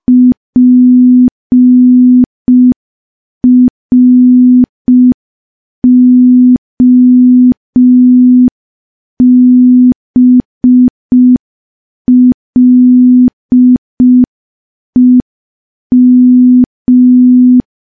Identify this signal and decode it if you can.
Morse code, heard as a keyed tone, about 5 wpm, 250 Hz. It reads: PROBLEM
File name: morse / morse-training-05WPM